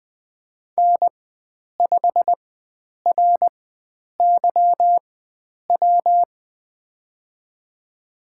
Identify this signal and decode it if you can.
Morse code, heard as a keyed tone, about 20 wpm, spaced as for 9 wpm, 700 Hz. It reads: N5RYW
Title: USA Call Signs - 5 Characters - 4x Character Spacing - 20wpm